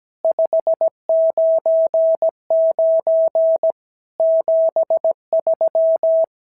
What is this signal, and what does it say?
Morse code, heard as a keyed tone, about 17 wpm, 650 Hz. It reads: 599 73
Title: SOTA QSO Practice 17wpm